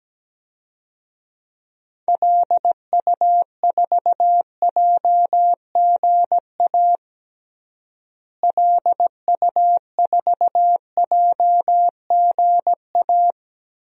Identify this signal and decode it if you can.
Morse code, heard as a keyed tone, about 17 wpm, 700 Hz. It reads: LU4JGA LU4JGA
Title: DXCC-100 Call Signs - 6 Characters - Repeated 3x - 17wpm